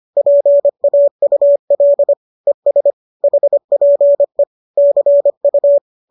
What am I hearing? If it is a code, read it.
Morse code, heard as a keyed tone, about 25 wpm, 570 Hz. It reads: PAUL ES HPE CU